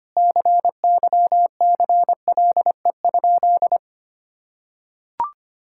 Morse code, heard as a keyed tone, about 25 wpm, 700 Hz. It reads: CYCLE?
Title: Sentences from Top 1000 Words - Review Part 6 - 25wpm